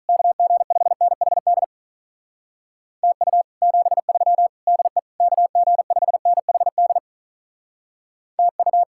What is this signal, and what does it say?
Morse code, heard as a keyed tone, about 46 wpm, 700 Hz. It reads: KG5NHD TU 73 DE KG5NHD TU